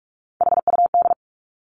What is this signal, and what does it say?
Morse code, heard as a keyed tone, about 46 wpm, 700 Hz. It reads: HUD
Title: Three Challenging Letter-Number Combinations 45wpm